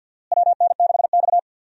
Morse code, heard as a keyed tone, about 52 wpm, 700 Hz. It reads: WN6X
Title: USA Call Signs - 4 Characters - Repeated 3x - 50wpm